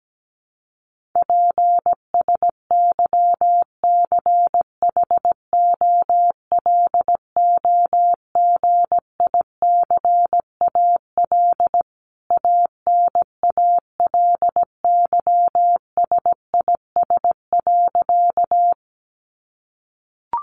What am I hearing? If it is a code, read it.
Morse code, heard as a keyed tone, about 17 wpm, 700 Hz. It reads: PSYCHOLOGICAL ANALYSIS.
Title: Sentences from Top 1400 Words 17wpm